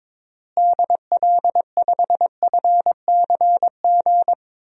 Morse code, heard as a keyed tone, about 22 wpm, 700 Hz. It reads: DL5FCG
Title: DXCC-100 Call Signs - 6 Characters - Repeated 2x - 22wpm